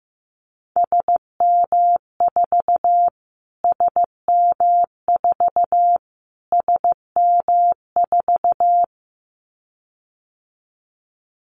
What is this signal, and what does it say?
Morse code, heard as a keyed tone, about 15 wpm, 700 Hz. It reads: SM4 SM4 SM4
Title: Callsign Prefixes for Top 25 DXCC - Repeated 3x - 15wpm